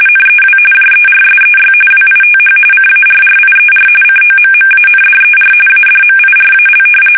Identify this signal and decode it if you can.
RTTY, about 45.45 baud. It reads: NEW JERSEY WITH CARRIER COMMANDER ADMIRAL
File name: fsk850 / battleshipnj